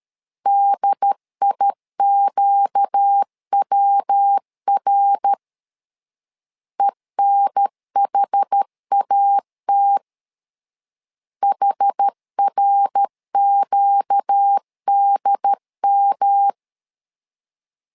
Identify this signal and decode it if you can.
Morse code, heard as a keyed tone, about 13 wpm, 800 Hz. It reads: DIQWR ENHAT HRQDM
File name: LEZIONI / 4 Q 12wpm 1833